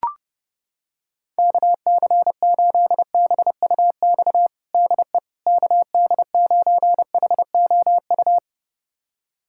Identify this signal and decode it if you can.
Morse code, heard as a keyed tone, about 30 wpm, 700 Hz. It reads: KC8BUX DE KD9HOU
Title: USA Callsign DE Callsign - 6 Characters - 30wpm